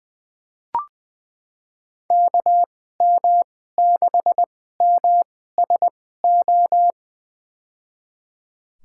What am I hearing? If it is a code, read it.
Morse code, heard as a keyed tone, about 20 wpm, spaced as for 15 wpm, 700 Hz. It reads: KM6MSO
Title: USA Call Signs - 6 Characters - 2x Character Spacing - 20wpm